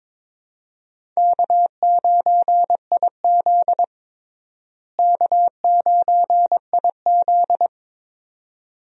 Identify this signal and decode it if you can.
Morse code, heard as a keyed tone, about 22 wpm, 700 Hz. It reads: K9IZ K9IZ
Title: USA Call Signs - 4 Characters - Repeated 2x - 22wpm